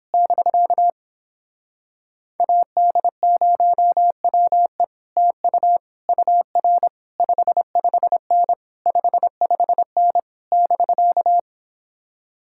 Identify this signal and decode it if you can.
Morse code, heard as a keyed tone, about 26 wpm, 700 Hz. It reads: <BK> AD0WE TU UR 55N 55N <BK>
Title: Speed-Racing Common QSO Phrases - 22wpm